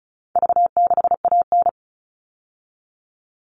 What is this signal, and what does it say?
Morse code, heard as a keyed tone, about 34 wpm, 700 Hz. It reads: V6AN
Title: All Cut Numbers - 4-Digit 35wpm